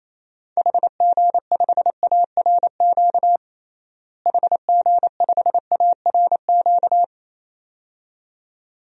Morse code, heard as a keyed tone, about 28 wpm, 700 Hz. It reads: HG5ARQ HG5ARQ
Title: DXCC-100 Call Signs - 6 Characters - Repeated 2x - 28wpm